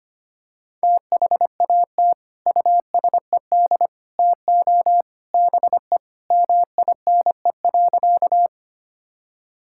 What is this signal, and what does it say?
Morse code, heard as a keyed tone, about 25 wpm, 700 Hz. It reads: THAT USED TO BE MINE.
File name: practice / 25wpm / Sentences with 3-Word Phrases - 25wpm